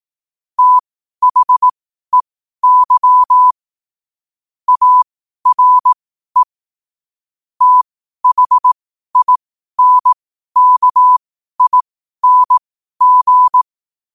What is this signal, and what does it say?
Morse code, heard as a keyed tone, about 18 wpm, spaced as for 12 wpm, 1000 Hz. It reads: THEY ARE THINKING
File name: qotdfiles / 12wpm